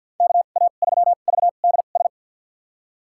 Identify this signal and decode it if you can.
Morse code, heard as a keyed tone, about 52 wpm, spaced as for 36 wpm, 700 Hz. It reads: KA3VDS
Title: USA Call Signs - 2x Character Spacing - 50wpm